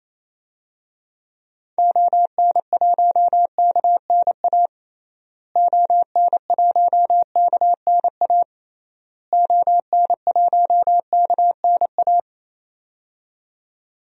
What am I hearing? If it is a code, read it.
Morse code, heard as a keyed tone, about 28 wpm, 700 Hz. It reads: ON1KNA ON1KNA ON1KNA
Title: DXCC-100 Call Signs - 6 Characters - Repeated 3x - 28wpm